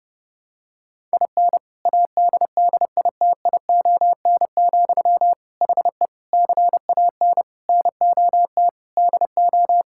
Morse code, heard as a keyed tone, about 32 wpm, 700 Hz. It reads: IN ADDITION, HE CAN NOT DO
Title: Sentences from Top 1000 Words - Review Part 6 - 30wpm